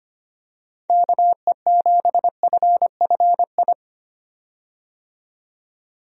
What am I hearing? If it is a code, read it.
Morse code, heard as a keyed tone, about 25 wpm, 700 Hz. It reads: KE7FFI
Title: USA Call Signs - 6 Characters - 25wpm